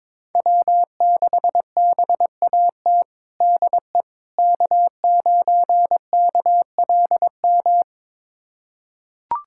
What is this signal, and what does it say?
Morse code, heard as a keyed tone, about 22 wpm, 700 Hz. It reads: W6BAT DE K9KLM
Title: USA Callsign DE Callsign - 5 Characters - 22wpm